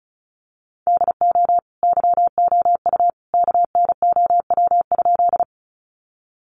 Morse code, heard as a keyed tone, about 34 wpm, 700 Hz. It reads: DO YOU KNOW?
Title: Sentences from Top 100 Words 35wpm